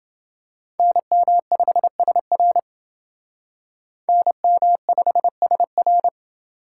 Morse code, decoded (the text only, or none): NM5SR NM5SR